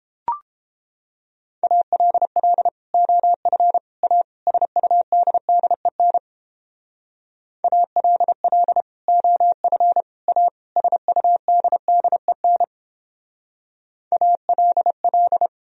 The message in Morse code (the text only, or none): ALL OF A SUDDEN ALL OF A SUDDEN ALL